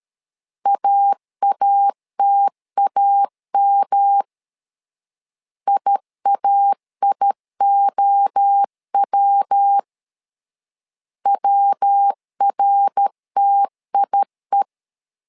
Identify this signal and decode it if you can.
Morse code, heard as a keyed tone, about 13 wpm, 800 Hz. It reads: AATAM IAIOW WRTIE
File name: LEZIONI / 3 MWRO / 3 MWRO 12wpm 1730